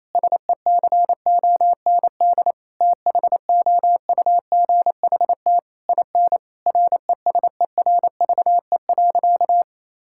Morse code, heard as a keyed tone, about 28 wpm, 700 Hz. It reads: SECOND THOUGHT IN RESERVE.
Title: Sentences from Top 1000 Words - Review Part 6 - 28wpm